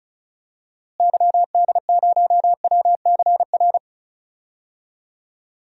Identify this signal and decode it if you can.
Morse code, heard as a keyed tone, about 36 wpm, 700 Hz. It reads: YD0WCR
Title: DXCC-100 Call Signs - 6 Characters - Repeated 3x - 35wpm